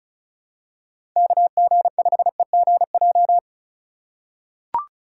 Morse code, heard as a keyed tone, about 36 wpm, 700 Hz. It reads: KG5EGJ
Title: USA Call Signs - 6 Characters - 3x Character Spacing - 35wpm